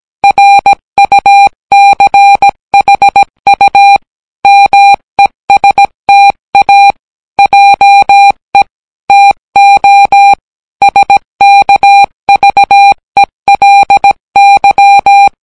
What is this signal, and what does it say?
Morse code, heard as a keyed tone, about 17 wpm, 800 Hz. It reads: RUCHU MESTA JE TO SKVELY